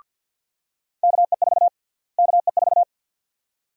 Morse code, heard as a keyed tone, about 52 wpm, 700 Hz. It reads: KE4 KE4
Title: USA Call Sign Prefixes - Repeated 2x - 50wpm